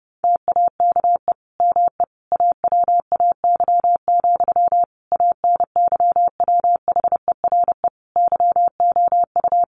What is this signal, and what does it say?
Morse code, heard as a keyed tone, about 30 wpm, 700 Hz. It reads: TAKE ME AWAY, ANYWHERE YOU
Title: Sentences from Top 1400 Words 30wpm